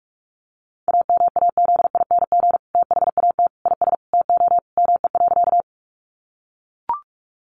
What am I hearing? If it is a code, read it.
Morse code, heard as a keyed tone, about 44 wpm, 700 Hz. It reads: AMAZING THAT IS TO ME.
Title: Sentences from Top 1500 Words 45wpm